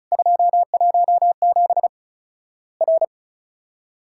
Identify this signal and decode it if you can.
Morse code, heard as a keyed tone, about 36 wpm, 700 Hz. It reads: 217 R
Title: Worked All Europe DX Contest - QTC Exchange - 35wpm